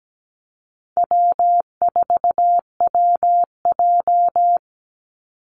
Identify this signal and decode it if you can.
Morse code, heard as a keyed tone, about 17 wpm, 700 Hz. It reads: W4WJ
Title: Call Signs Repeated 2x - 17wpm